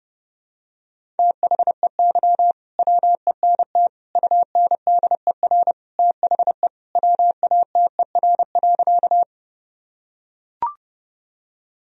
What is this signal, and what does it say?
Morse code, heard as a keyed tone, about 31 wpm, 700 Hz. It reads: THEY WENT UNDER THE WATER.
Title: Sentences from Top 1000 Words - Review Part 1 - 30wpm